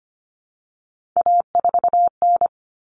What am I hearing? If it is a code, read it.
Morse code, heard as a keyed tone, about 25 wpm, 700 Hz. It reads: A4N
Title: Common Cut Numbers - 3-Digit 25wpm